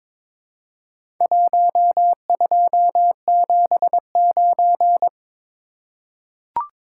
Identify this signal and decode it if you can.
Morse code, heard as a keyed tone, about 23 wpm, 700 Hz. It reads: 1279
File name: practice / 22wpm / Four Letter-Number Combinations - 22wpm